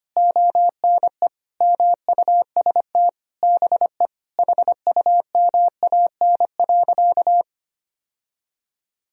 Morse code, heard as a keyed tone, about 25 wpm, 700 Hz. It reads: ONE MUST BE HUMAN.
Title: Sentences from Top 1000 Words - Review Part 3 - 25wpm